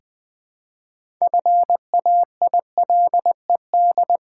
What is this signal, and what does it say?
Morse code, heard as a keyed tone, about 20 wpm, 700 Hz. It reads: FAILED
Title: 6 Letter Words - First 3 Letters Given - 20wpm